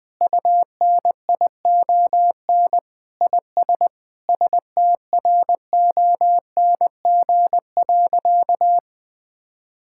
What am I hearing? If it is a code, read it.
Morse code, heard as a keyed tone, about 21 wpm, 700 Hz. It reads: UNION IS STRONG.
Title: Sentences with 3-Word Phrases - 20wpm